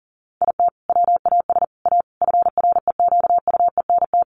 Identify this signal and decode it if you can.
Morse code, heard as a keyed tone, about 40 wpm, 700 Hz. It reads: IT WAS A FREQUENT
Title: Sentences from Top 1400 Words 40wpm